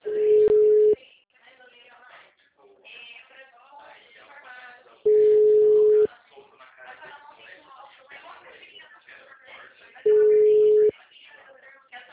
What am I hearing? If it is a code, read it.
Ringback tone (Europe, 425 Hz), with speech.